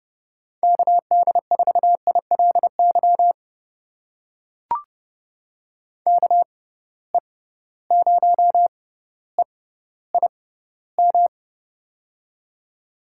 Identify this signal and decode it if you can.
Morse code, heard as a keyed tone, about 30 wpm, spaced as for 15 wpm, 700 Hz. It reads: KD4ILY KE0EIM
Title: USA Call Signs - 6 Characters - 6x Character Spacing - 30wpm